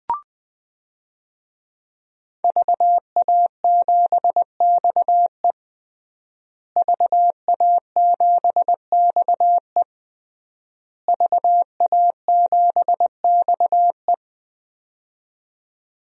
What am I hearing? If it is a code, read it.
Morse code, heard as a keyed tone, about 20 wpm, 700 Hz. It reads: VA7XE VA7XE VA7XE
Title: DXCC-100 Call Signs - 5 Characters - Repeated 3x - 20wpm